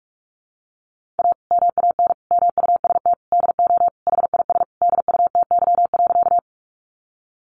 Morse code, heard as a keyed tone, about 45 wpm, 700 Hz. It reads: A MAN MUST DO HIS DUTY.
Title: Sentences from Top 1500 Words 45wpm